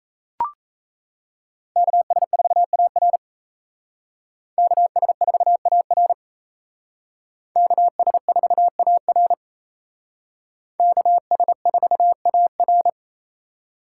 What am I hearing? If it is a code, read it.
Morse code, heard as a keyed tone, about 32 wpm, 700 Hz. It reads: KS4AR KS4AR KS4AR KS4AR
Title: Speed Racing - USA 2x2 Call Signs 28wpm